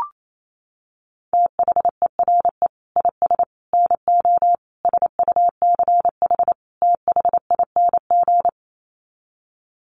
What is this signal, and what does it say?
Morse code, heard as a keyed tone, about 28 wpm, 690 Hz. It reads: THERE IS NO SUCH THING
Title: Sets of 5 Words Encore 28wpm